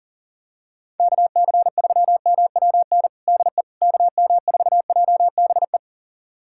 Morse code, heard as a keyed tone, about 40 wpm, 700 Hz. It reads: KC3MWN DE KM4JBE
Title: USA Callsign DE Callsign - 6 Characters - 40wpm